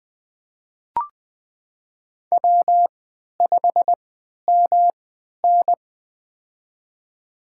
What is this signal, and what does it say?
Morse code, heard as a keyed tone, about 20 wpm, spaced as for 11 wpm, 700 Hz. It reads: W5MN
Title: USA Call Signs - 4 Characters - 3x Character Spacing - 20wpm